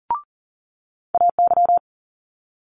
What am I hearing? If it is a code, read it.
Morse code, heard as a keyed tone, about 43 wpm, 700 Hz. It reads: AY
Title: Multi-letter Phonemes Quick Fire 40wpm